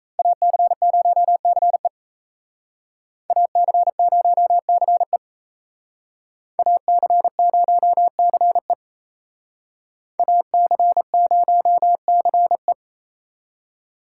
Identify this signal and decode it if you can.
Morse code, heard as a keyed tone, about 35 wpm, 700 Hz. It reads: AC0CE AC0CE AC0CE AC0CE